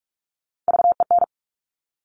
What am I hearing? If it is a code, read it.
Morse code, heard as a keyed tone, about 46 wpm, 700 Hz. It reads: VEN
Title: Top Three Letter Combinations in Words 45wpm